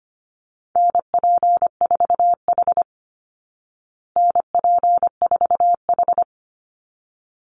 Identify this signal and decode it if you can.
Morse code, heard as a keyed tone, about 25 wpm, 700 Hz. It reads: NP4H NP4H
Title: Call Signs Repeated 3x - 25wpm